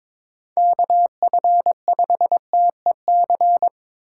Morse code, heard as a keyed tone, about 22 wpm, 700 Hz. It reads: KF5TEC
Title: USA Call Signs - 6 Characters - 3x Character Spacing - 22wpm